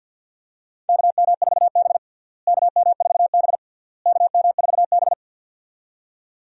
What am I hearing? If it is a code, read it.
Morse code, heard as a keyed tone, about 52 wpm, 700 Hz. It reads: KM4B KM4B KM4B
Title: USA Call Signs - 4 Characters - Repeated 3x - 50wpm